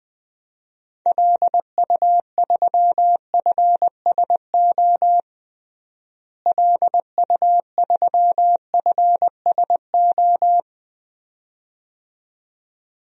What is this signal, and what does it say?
Morse code, heard as a keyed tone, about 20 wpm, 700 Hz. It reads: LU3FSO LU3FSO
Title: DXCC-100 Call Signs - 6 Characters - Repeated 2x - 20wpm